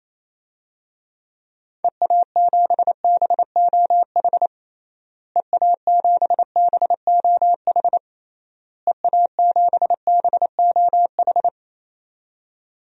Morse code, heard as a keyed tone, about 28 wpm, 700 Hz. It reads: EA7BOH EA7BOH EA7BOH
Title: DXCC-100 Call Signs - 6 Characters - Repeated 3x - 28wpm